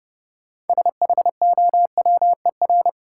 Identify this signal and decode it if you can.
Morse code, heard as a keyed tone, about 32 wpm, 700 Hz. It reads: SHOWER
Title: Sentences from Top 1600 Words - New Word Drill Alt - 30wpm